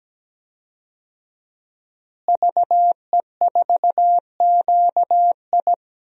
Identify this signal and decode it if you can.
Morse code, heard as a keyed tone, about 17 wpm, 700 Hz. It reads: VE4QI
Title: DXCC-100 Call Signs - 5 Characters - Repeated 3x - 17wpm